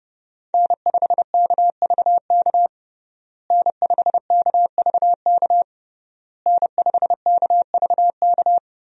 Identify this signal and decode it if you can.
Morse code, heard as a keyed tone, about 30 wpm, 700 Hz. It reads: N5KVK N5KVK N5KVK
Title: USA Call Signs - 5 Characters - Repeated 3x - 30wpm